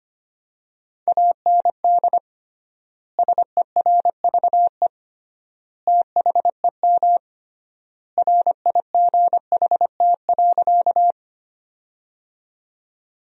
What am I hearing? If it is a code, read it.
Morse code, heard as a keyed tone, about 25 wpm, 700 Hz. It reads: AND SERVE THEM RIGHT.
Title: 3x Word-Spacing - Sentences from Top 400 Words - 25wpm